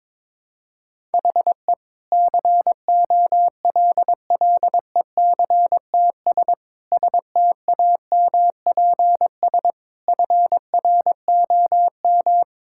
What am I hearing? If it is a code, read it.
Morse code, heard as a keyed tone, about 22 wpm, 700 Hz. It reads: HE COLLECTS STAMPS FROM
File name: Difficult Sentences with Hints - 22wpm